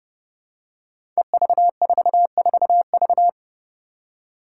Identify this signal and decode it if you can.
Morse code, heard as a keyed tone, about 30 wpm, 700 Hz. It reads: EV44V
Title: Five Challenging Letter-Number Combinations 30wpm